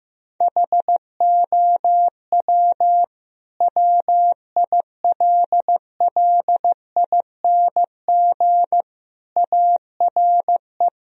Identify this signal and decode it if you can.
Morse code, heard as a keyed tone, about 15 wpm, 700 Hz. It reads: HOW WILLING ARE
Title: Sentences from Top 1000 Words - Review Part 3 - 15wpm